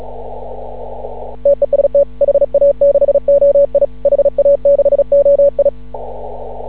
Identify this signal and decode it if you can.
Morse code, heard as a keyed tone, about 36 wpm, 560 Hz. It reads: TEST HA6OI HA6OI
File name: CW / ha6oi 2